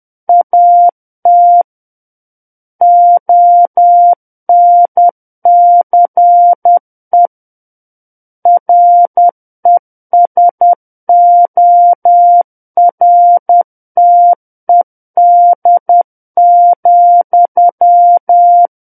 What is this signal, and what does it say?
Morse code, heard as a keyed tone, about 10 wpm, 700 Hz. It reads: AT ONCE RESORTED,
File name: Book One / 01-Chapter 1 - 10 WPM